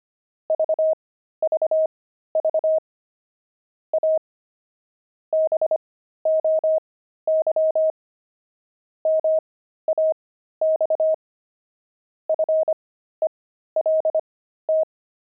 Morse code, heard as a keyed tone, about 25 wpm, spaced as for 13 wpm, 630 Hz. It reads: VVV A BOY MAX FELT
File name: pangram-fs-13-cs-25-weaving